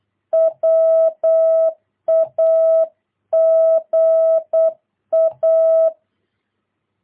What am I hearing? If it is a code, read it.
Morse code, heard as a keyed tone, about 8 wpm, 660 Hz. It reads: WAGA
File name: jh8cbh / 2022cw